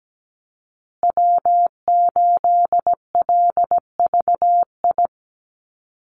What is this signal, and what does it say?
Morse code, heard as a keyed tone, about 17 wpm, 700 Hz. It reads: W8LVI
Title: Call Signs Repeated 2x - 17wpm